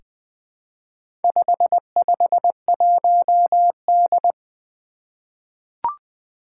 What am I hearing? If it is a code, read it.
Morse code, heard as a keyed tone, about 20 wpm, 700 Hz. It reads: 551D
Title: Four Challenging Letter-Number Combinations 20wpm